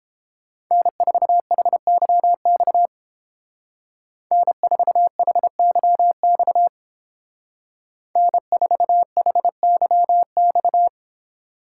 Morse code, heard as a keyed tone, about 29 wpm, 700 Hz. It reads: N4HYX N4HYX N4HYX